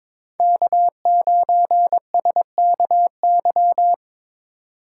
Morse code, heard as a keyed tone, about 25 wpm, 700 Hz. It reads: K9SKY
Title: Speed Racing - USA 1x3 Call Signs 22wpm